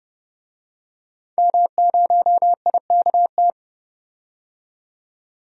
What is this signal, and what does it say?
Morse code, heard as a keyed tone, about 30 wpm, 700 Hz. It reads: M0IKT
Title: DXCC-100 Call Signs - 5 Characters - 30wpm